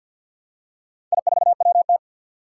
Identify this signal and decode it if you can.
Morse code, heard as a keyed tone, about 50 wpm, 700 Hz. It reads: I4WT
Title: Four Challenging Letter-Number Combinations 50wpm